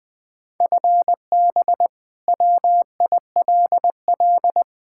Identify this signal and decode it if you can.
Morse code, heard as a keyed tone, about 20 wpm, 700 Hz. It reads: FB WILL
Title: Practice QSO - Exchange 3 - More Info - 20wpm